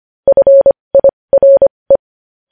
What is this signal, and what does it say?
Morse code, heard as a keyed tone, about 25 wpm, spaced as for 20 wpm, 550 Hz. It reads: FIRE
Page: News Headlines in Morse Code at 20 WPM